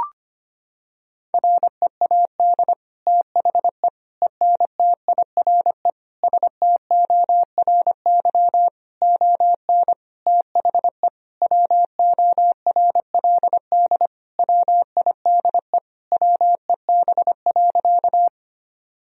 Morse code, heard as a keyed tone, about 25 wpm, 700 Hz. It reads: READ THE ENTIRE STORY ON THE WORLD WIDE WEB.
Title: Sentences from Top 1000 Words - Review Part 4 - 25wpm